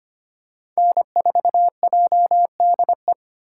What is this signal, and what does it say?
Morse code, heard as a keyed tone, about 25 wpm, 700 Hz. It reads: N4JDE
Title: USA Call Signs - 5 Characters - 2x Character Spacing - 25wpm